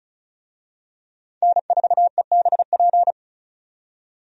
Morse code, heard as a keyed tone, about 36 wpm, 700 Hz. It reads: N4EBP
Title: USA Call Signs - 5 Characters - 35wpm